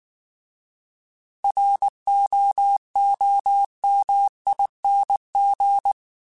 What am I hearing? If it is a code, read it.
Morse code, heard as a keyed tone, about 19 wpm, 800 Hz. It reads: ROOMING
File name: Alphabetical / Alphabetical mp3s / Rs @18